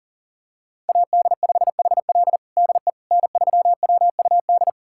Morse code, heard as a keyed tone, about 40 wpm, 700 Hz. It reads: AD5HL DE N3WUD
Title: USA Callsign DE Callsign - 5 Characters - 40wpm